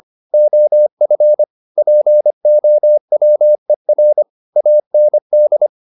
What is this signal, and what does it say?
Morse code, heard as a keyed tone, about 25 wpm, 600 Hz. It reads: OF POWER AND